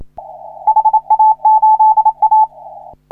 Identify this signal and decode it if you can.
Morse code, heard as a keyed tone, about 28 wpm, 870 Hz. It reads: HA8A
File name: CW / HA3OU RHOBCW REC